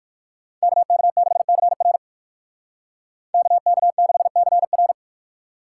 Morse code, heard as a keyed tone, about 64 wpm, spaced as for 56 wpm, 700 Hz. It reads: KK6CR KK6CR
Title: Speed Racing - USA 2x2 Call Signs 45wpm